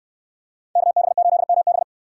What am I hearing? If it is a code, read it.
Morse code, heard as a keyed tone, about 60 wpm, 700 Hz. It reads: KB8MB